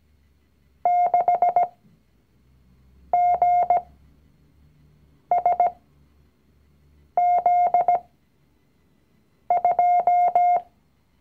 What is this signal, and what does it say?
Morse code, heard as a keyed tone, about 17 wpm, spaced as for 5 wpm, 710 Hz. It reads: 6GSZ2
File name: CW-Online-Kurs / Anfänger / Lektion 019